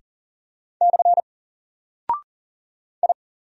Morse code, heard as a keyed tone, about 40 wpm, 700 Hz. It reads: / I
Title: Single Character (Letter:Number:Punctuation:Prosign) 40wpm